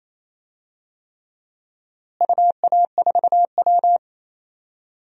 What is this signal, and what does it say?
Morse code, heard as a keyed tone, about 28 wpm, 700 Hz. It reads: UA4W
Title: DXCC-100 Call Signs - 4 Characters - Repeated 3x - 28wpm